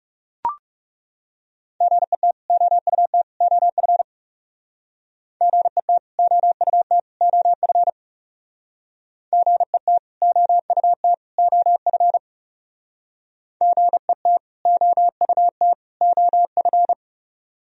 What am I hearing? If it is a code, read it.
Morse code, heard as a keyed tone, about 37 wpm, 700 Hz. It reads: GET OUT OF GET OUT OF GET OUT OF GET OUT OF
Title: Speed-Racing - Sets of 3 Words 30wpm